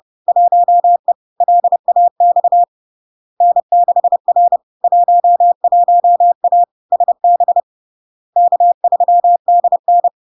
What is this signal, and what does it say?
Morse code, heard as a keyed tone, about 30 wpm, 700 Hz. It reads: 1E LAX N6R 11A SB K3DN
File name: LARK-FD-2022-Log-in-CW-30wpm-700-Hz